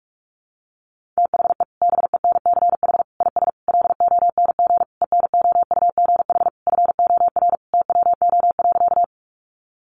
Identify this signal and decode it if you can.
Morse code, heard as a keyed tone, about 44 wpm, 700 Hz. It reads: THE BENCH IS LONG ENOUGH FOR TWO.
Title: Sentences from Top 1400 Words 45wpm